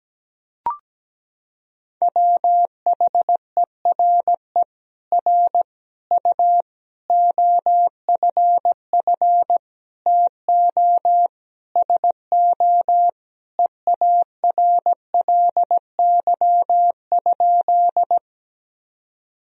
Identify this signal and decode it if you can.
Morse code, heard as a keyed tone, about 17 wpm, 700 Hz. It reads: WHERE R U OFF TO SO EARLY?
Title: Sentences from Top 400 Words and Abbreviations 17wpm